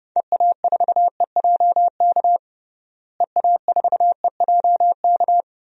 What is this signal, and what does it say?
Morse code, heard as a keyed tone, about 32 wpm, 700 Hz. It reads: EA4EJK EA4EJK
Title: DXCC-100 Call Signs - 6 Characters - Repeated 2x - 30wpm